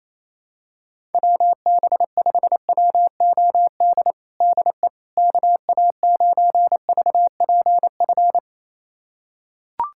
Morse code, heard as a keyed tone, about 28 wpm, 700 Hz. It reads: WB5WOD DE KA9VPF
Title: USA Callsign DE Callsign - 6 Characters - 28wpm